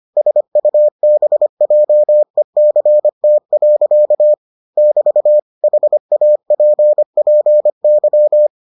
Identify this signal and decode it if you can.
Morse code, heard as a keyed tone, about 25 wpm, 600 Hz. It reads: SUBJECT. <BT> HAPPY